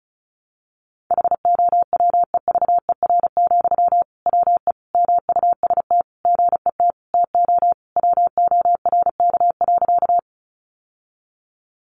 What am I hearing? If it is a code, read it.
Morse code, heard as a keyed tone, about 34 wpm, 700 Hz. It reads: HOWEVER, WE MUST GET TO WORK.
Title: Sentences from Top 600 Words 35wpm